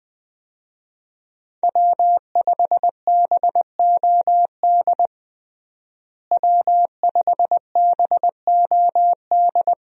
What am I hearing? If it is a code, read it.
Morse code, heard as a keyed tone, about 20 wpm, 700 Hz. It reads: W5BOD W5BOD
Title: USA Call Signs - 5 Characters - Repeated 3x - 20wpm